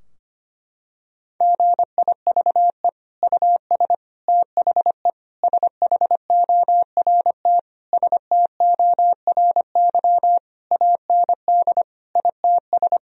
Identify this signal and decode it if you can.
Morse code, heard as a keyed tone, about 25 wpm, 700 Hz. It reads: GIVE US THE SHORT STORY AND ITS
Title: Sentences from Top 1000 Words - Review Part 4 - 25wpm